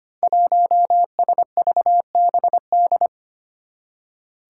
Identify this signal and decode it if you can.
Morse code, heard as a keyed tone, about 25 wpm, 700 Hz. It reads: 1SVBD
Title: Five Challenging Letter-Number Combinations 25wpm